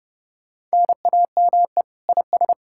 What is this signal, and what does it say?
Morse code, heard as a keyed tone, about 40 wpm, spaced as for 35 wpm, 700 Hz. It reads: NAME IS
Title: Speed-Racing Common QSO Phrases - 30wpm